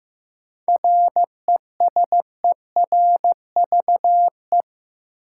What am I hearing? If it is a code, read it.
Morse code, heard as a keyed tone, about 16 wpm, 700 Hz. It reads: RESERVE
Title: Sentences from Top 1000 Words - Review Part 6 - New Word Drill - 15wpm